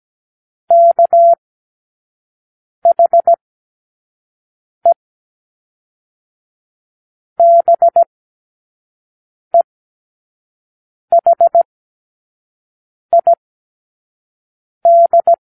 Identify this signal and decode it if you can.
Morse code, heard as a keyed tone, about 17 wpm, 680 Hz. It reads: K H E B E H I D
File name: k7qo / Lesson18(A-L)